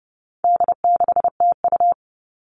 Morse code, heard as a keyed tone, about 30 wpm, 700 Hz. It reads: D6TU
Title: All Cut Numbers - 4-Digit 30wpm